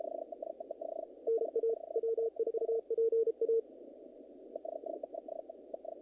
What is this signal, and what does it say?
Morse code, heard as a keyed tone, about 31 wpm, 460 Hz. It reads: NA W4PA
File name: contests / naqp-cw-k5tr-op-k5pi- -2000